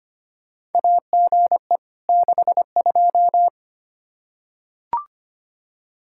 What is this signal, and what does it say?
Morse code, heard as a keyed tone, about 25 wpm, 700 Hz. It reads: AGE 62
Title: Practice QSO - Exchange 3 - More Info - 25wpm